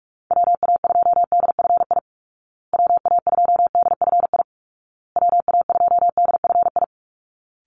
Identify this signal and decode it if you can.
Morse code, heard as a keyed tone, about 44 wpm, 700 Hz. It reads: WA2DFI WA2DFI WA2DFI
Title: Call Signs Repeated 2x - 45wpm